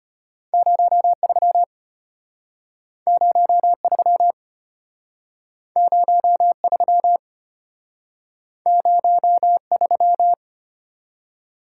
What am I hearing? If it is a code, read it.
Morse code, heard as a keyed tone, about 34 wpm, 700 Hz. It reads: 03 03 03 03